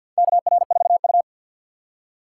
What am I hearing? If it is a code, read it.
Morse code, heard as a keyed tone, about 52 wpm, 700 Hz. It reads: KR4U
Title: USA Call Signs - 4 Characters - 6x Character Spacing - 50wpm